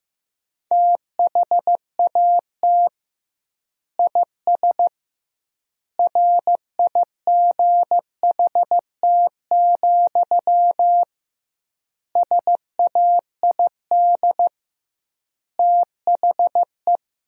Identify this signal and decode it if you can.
Morse code, heard as a keyed tone, about 16 wpm, 700 Hz. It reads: THAT IS RIGHT, SAID THE
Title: 2x Word-Spacing - Sentences from Top 200 Words - 15wpm